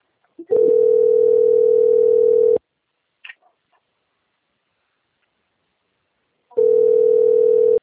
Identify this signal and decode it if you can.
Ringback tone (North America), 440 + 480 Hz.